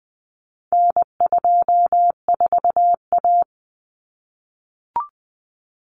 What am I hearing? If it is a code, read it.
Morse code, heard as a keyed tone, about 20 wpm, 700 Hz. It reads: N24A
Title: Common Cut Numbers - 4-Digit 20wpm